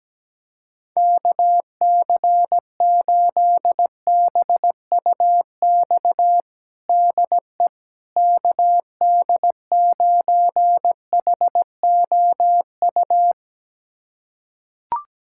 Morse code, heard as a keyed tone, about 17 wpm, 700 Hz. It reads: KC8BUX DE KD9HOU
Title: USA Callsign DE Callsign - 6 Characters - 17wpm